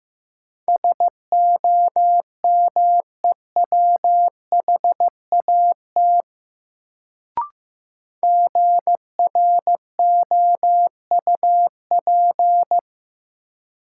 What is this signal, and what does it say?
Morse code, heard as a keyed tone, about 16 wpm, 700 Hz. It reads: SOMEWHAT GROUP
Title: Sentences from Top 1000 Words - Review Part 1 - New Word Drill - 15wpm